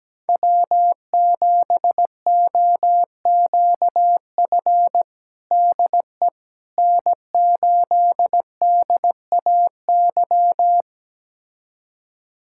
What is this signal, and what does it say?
Morse code, heard as a keyed tone, about 17 wpm, 700 Hz. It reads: W7OQF DE N8DAY
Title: USA Callsign DE Callsign - 5 Characters - 17wpm